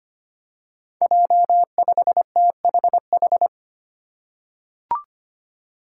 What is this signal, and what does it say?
Morse code, heard as a keyed tone, about 25 wpm, 700 Hz. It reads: J5THH
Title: Five Challenging Letter-Number Combinations 25wpm